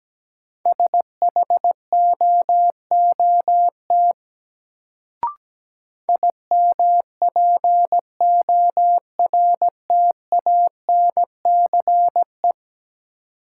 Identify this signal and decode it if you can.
Morse code, heard as a keyed tone, about 17 wpm, 700 Hz. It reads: SHOOT IMPORTANCE
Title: Sentences from Top 1200 Words - New Word Drill - 17wpm